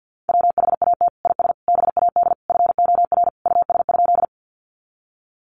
Morse code, heard as a keyed tone, about 50 wpm, 700 Hz. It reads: WHAT IS BAD FOR US?
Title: Sentences from Top 700 Words 50wpm